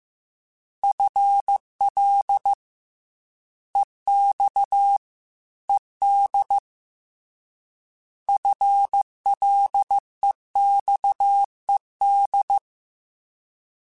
Morse code, heard as a keyed tone, about 15 wpm, 800 Hz. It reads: FL EX ED FLEXED
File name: FL / FL mp3s / Fast FL @13